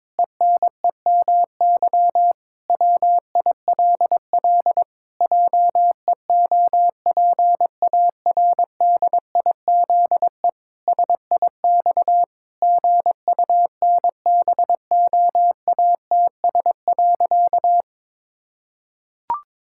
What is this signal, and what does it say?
Morse code, heard as a keyed tone, about 22 wpm, 700 Hz. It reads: ENEMY WILL JEOPARDIZE SIX GUNBOATS.
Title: Pangram Challenge v2 - 22wpm